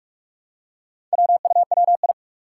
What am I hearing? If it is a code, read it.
Morse code, heard as a keyed tone, about 44 wpm, 700 Hz. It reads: WUWI
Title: Four Challenging Letter-Number Combinations 45wpm